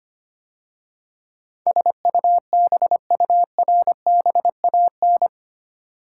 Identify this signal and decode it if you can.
Morse code, heard as a keyed tone, about 25 wpm, 700 Hz. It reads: SUBURBAN
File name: 8 Letter Words - First 3 Letters Given - 25wpm